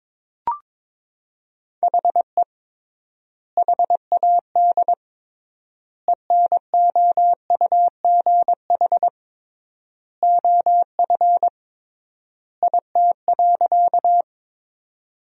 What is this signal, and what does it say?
Morse code, heard as a keyed tone, about 21 wpm, 700 Hz. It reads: HE HAD ENOUGH OF IT.
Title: 3x Word-Spacing - Sentences from Top 400 Words - 22wpm